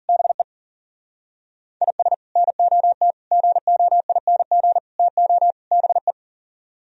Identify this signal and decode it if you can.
Morse code, heard as a keyed tone, about 51 wpm, 700 Hz. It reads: BE IS NOT GOING TO BE